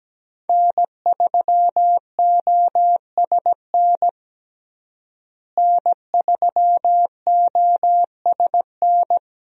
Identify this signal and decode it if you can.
Morse code, heard as a keyed tone, about 17 wpm, 700 Hz. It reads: N3OSN N3OSN
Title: USA Call Signs - 5 Characters - Repeated 2x - 17wpm